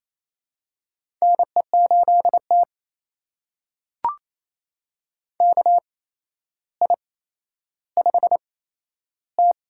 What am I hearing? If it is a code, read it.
Morse code, heard as a keyed tone, about 28 wpm, spaced as for 12 wpm, 700 Hz. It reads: NE8T KI5T
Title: USA Call Signs - 4 Characters - 8x Character Spacing - 28wpm